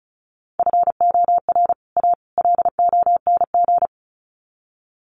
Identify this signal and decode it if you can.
Morse code, heard as a keyed tone, about 34 wpm, 700 Hz. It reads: FOR A LONG
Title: Sets of 3 Words 35wpm